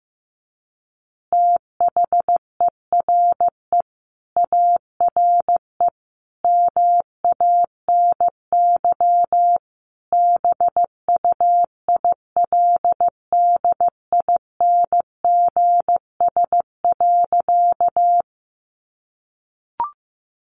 Morse code, heard as a keyed tone, about 15 wpm, 700 Hz. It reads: THERE ARE MANY BUILDINGS.
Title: Easy Phrases 15wpm